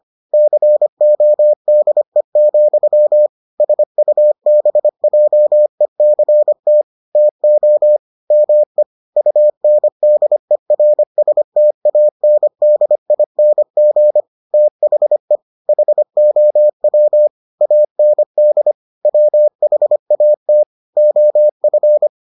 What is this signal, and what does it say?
Morse code, heard as a keyed tone, about 25 wpm, 600 Hz. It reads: CODE, SUBJECT TO ME UNDERSTANDING THE HOW AND WHAT OF